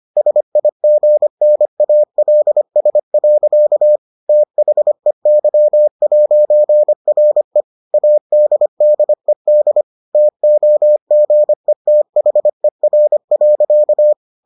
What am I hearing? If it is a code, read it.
Morse code, heard as a keyed tone, about 25 wpm, 600 Hz. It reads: SIGNALS. THEY'RE ADDED TOGETHER.